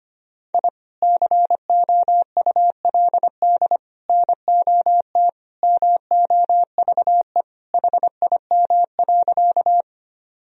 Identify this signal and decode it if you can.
Morse code, heard as a keyed tone, about 25 wpm, 700 Hz. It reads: I COULD NOT MOVE HIM.
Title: Sentences from Top 1000 Words - Review Part 1 - 25wpm